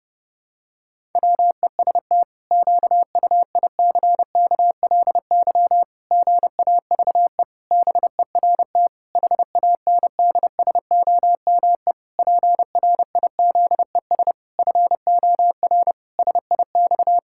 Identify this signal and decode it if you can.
Morse code, heard as a keyed tone, about 32 wpm, 700 Hz. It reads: WEST QUICKLY GAVE BERT HANDSOME PRIZES FOR SIX
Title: Pangram Challenge v2 - 30wpm